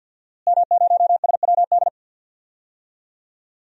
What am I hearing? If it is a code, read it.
Morse code, heard as a keyed tone, about 48 wpm, 700 Hz. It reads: M0SWD
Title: DXCC-100 Call Signs - 5 Characters - Repeated 2x - 50wpm